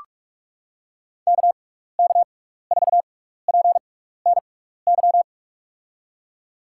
Morse code, heard as a keyed tone, about 47 wpm, spaced as for 16 wpm, 700 Hz. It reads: KK4PNY
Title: USA Call Signs - 6 Characters - 6x Character Spacing - 45wpm